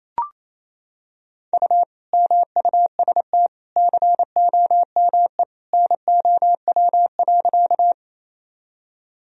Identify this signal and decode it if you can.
Morse code, heard as a keyed tone, about 28 wpm, 700 Hz. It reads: U MUST COME NOW.
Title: Sentences from Top 200 Words and Abbreviations 28wpm